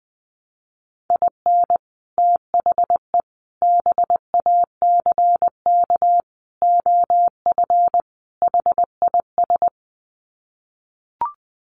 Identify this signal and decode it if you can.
Morse code, heard as a keyed tone, about 20 wpm, 700 Hz. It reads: IN THE BACK OF HIS
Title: Sets of 5 Words Encore2 20wpm